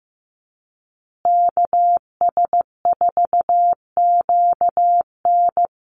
Morse code, heard as a keyed tone, about 15 wpm, 700 Hz. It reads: KS4QN
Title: USA Call Signs Repeated 3x 15wpm